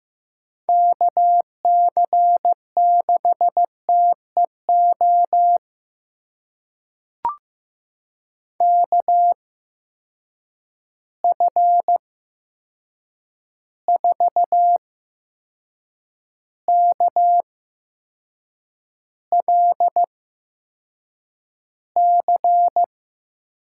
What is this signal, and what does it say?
Morse code, heard as a keyed tone, about 15 wpm, spaced as for 6 wpm, 700 Hz. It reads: KC6TEO KF4KLC
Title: USA Call Signs - 6 Characters - 8x Character Spacing - 15wpm